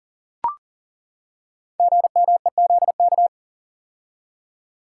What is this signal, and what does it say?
Morse code, heard as a keyed tone, about 40 wpm, 700 Hz. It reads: GMEZK
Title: Code Groups - 5 Letter:Numbers - 40wpm